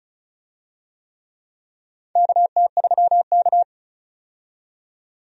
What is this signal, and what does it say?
Morse code, heard as a keyed tone, about 36 wpm, 700 Hz. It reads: KT3K
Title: USA Call Signs - 4 Characters - 35wpm